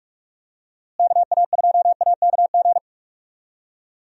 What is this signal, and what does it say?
Morse code, heard as a keyed tone, about 46 wpm, 700 Hz. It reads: KA2AKG
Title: USA Call Signs - 6 Characters - Repeated 3x - 45wpm